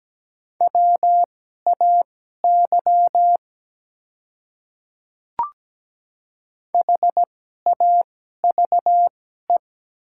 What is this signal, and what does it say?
Morse code, heard as a keyed tone, about 17 wpm, spaced as for 12 wpm, 700 Hz. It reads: WAY HAVE
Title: 2x Character Spacing - Top 200 Words - 17wpm